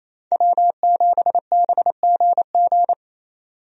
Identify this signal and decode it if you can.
Morse code, heard as a keyed tone, about 28 wpm, 700 Hz. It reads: W7BGG
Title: USA Call Signs - 5 Characters - 28wpm